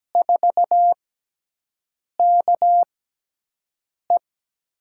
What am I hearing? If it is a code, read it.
Morse code, heard as a keyed tone, about 17 wpm, spaced as for 6 wpm, 700 Hz. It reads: 4KE
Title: USA Call Signs - 6x Character Spacing - 17wpm